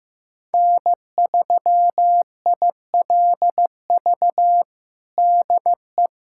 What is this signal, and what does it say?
Morse code, heard as a keyed tone, about 15 wpm, 700 Hz. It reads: N3ILV DE
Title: USA Callsign DE Callsign 15wpm